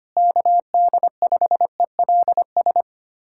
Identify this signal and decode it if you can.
Morse code, heard as a keyed tone, about 25 wpm, 700 Hz. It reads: KD5ELS
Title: USA Callsign DE Callsign 25wpm